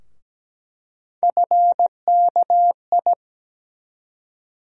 Morse code, heard as a keyed tone, about 17 wpm, 700 Hz. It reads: FKI
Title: USA Call Sign Postfixes - 3 Letters 17wpm